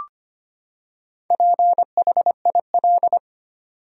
Morse code, heard as a keyed tone, about 25 wpm, 700 Hz. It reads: PHIL
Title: USA Common Names - Max 4 Letters - 25wpm